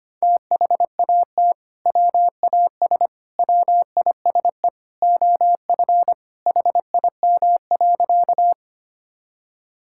Morse code, heard as a keyed tone, about 25 wpm, 700 Hz. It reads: THAT WAS WISE OF HIM.
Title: Sentences from Top 1600 Words 25wpm